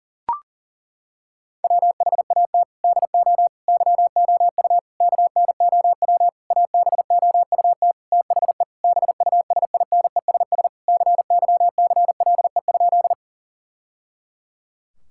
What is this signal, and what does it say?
Morse code, heard as a keyed tone, about 40 wpm, 700 Hz. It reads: WHAT DO YOU KNOW ABOUT THE BUSINESS CYCLE?
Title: Sentences from Top 1000 Words - Review Part 6 - 40wpm